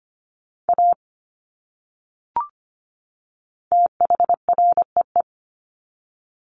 Morse code, heard as a keyed tone, about 25 wpm, 700 Hz. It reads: A THREE
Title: Top 500 Words 25wpm